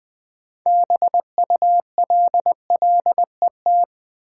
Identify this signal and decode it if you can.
Morse code, heard as a keyed tone, about 20 wpm, 700 Hz. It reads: BULLET
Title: Build A Word - 2 Letters At A Time - 6-Letter Words - 20wpm